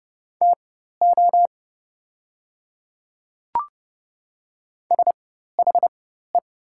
Morse code, heard as a keyed tone, about 30 wpm, spaced as for 14 wpm, 700 Hz. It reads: TO SHE
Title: 4x Character Spacing - Top 100 Words - 30wpm